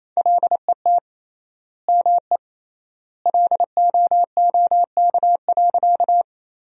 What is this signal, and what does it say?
Morse code, heard as a keyed tone, about 28 wpm, 700 Hz. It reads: LET ME LOOK.
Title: 3x Word-Spacing - Sentences from Top 300 Words - 28wpm